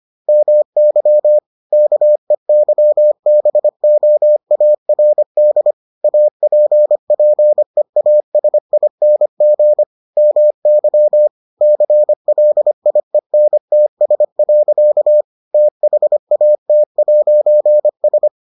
Morse code, heard as a keyed tone, about 25 wpm, 600 Hz. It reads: MY KEYBOARD APPEASING MY CLIENTS. THAT'S